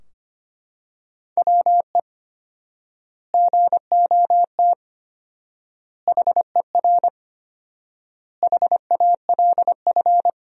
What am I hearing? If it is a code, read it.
Morse code, heard as a keyed tone, about 25 wpm, 700 Hz. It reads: WE GOT HER HALF
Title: 4x Word-Spacing - Sentences from Top 400 Words - 25wpm